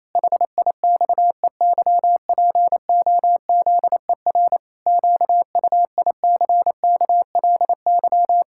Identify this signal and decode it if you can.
Morse code, heard as a keyed tone, about 28 wpm, 700 Hz. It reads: HIXEYPOZER QUICKLY
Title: Pangram Challenge v2 - 28wpm